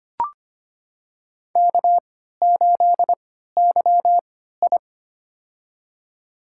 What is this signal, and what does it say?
Morse code, heard as a keyed tone, about 25 wpm, spaced as for 14 wpm, 700 Hz. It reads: K8YI
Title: USA Call Signs - 4 Characters - 3x Character Spacing - 25wpm